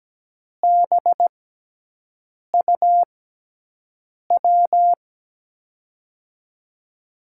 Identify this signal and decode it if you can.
Morse code, heard as a keyed tone, about 17 wpm, spaced as for 6 wpm, 700 Hz. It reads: BUW
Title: USA Call Signs - 6 Characters - 6x Character Spacing - 17wpm